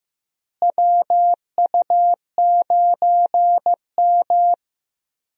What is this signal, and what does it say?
Morse code, heard as a keyed tone, about 15 wpm, 700 Hz. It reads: WU9M
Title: USA Call Signs - 4 Characters - Repeated 2x - 15wpm